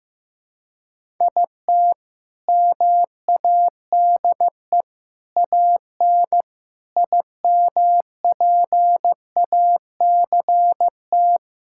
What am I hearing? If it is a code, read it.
Morse code, heard as a keyed tone, about 15 wpm, 700 Hz. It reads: IT MADE AN IMPACT
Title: Sentences from Top 1000 Words - Review Part 6 - 15wpm